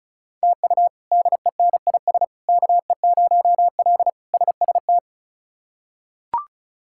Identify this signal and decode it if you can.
Morse code, heard as a keyed tone, about 36 wpm, 700 Hz. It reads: TU DENIS KE0L SST
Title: K1USN SST QSO Practice - 35wpm